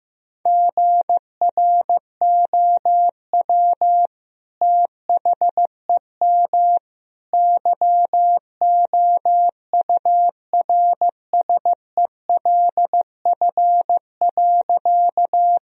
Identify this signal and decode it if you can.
Morse code, heard as a keyed tone, about 15 wpm, 700 Hz. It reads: GROW THEM YOURSELF.
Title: Sentences from Top 1000 Words - Review Part 1 - 15wpm